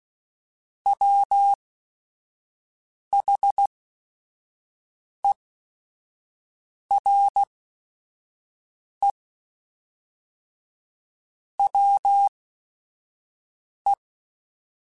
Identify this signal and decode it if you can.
Morse code, heard as a keyed tone, about 16 wpm, spaced as for 5 wpm, 800 Hz. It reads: WHERE WE
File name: Technical Subjects / Technical Subjects-mp3s / Sw Pwr Sup @5